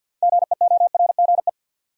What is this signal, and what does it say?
Morse code, heard as a keyed tone, about 52 wpm, 700 Hz. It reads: GEORGE
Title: USA Common Names - 50wpm